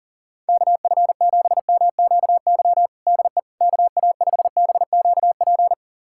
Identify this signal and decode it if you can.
Morse code, heard as a keyed tone, about 40 wpm, 700 Hz. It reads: KF7MQY DE KA5BQP
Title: USA Callsign DE Callsign - 6 Characters - 40wpm